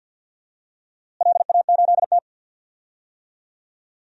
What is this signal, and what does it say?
Morse code, heard as a keyed tone, about 52 wpm, 700 Hz. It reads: LA8T
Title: DXCC-100 Call Signs - 4 Characters - Repeated 3x - 50wpm